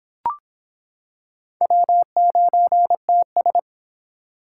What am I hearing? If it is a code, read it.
Morse code, heard as a keyed tone, about 26 wpm, 700 Hz. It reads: W9TS